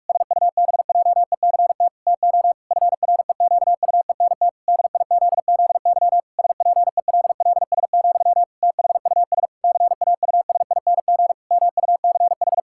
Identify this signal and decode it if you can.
Morse code, heard as a keyed tone, about 48 wpm, 700 Hz. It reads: SUBJECT TO FREQUENT DIZZY SPELLS, THUS CAUSING MUCH
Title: Pangram Challenge v2 - 45wpm